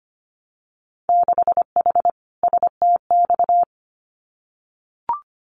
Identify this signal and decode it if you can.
Morse code, heard as a keyed tone, about 25 wpm, 700 Hz. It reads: 6H STX
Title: Winter Field Day Exchanges Encore 25wpm